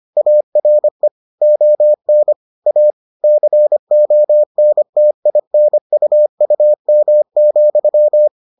Morse code, heard as a keyed tone, about 25 wpm, 600 Hz. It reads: ARE ON A CONTINUUM,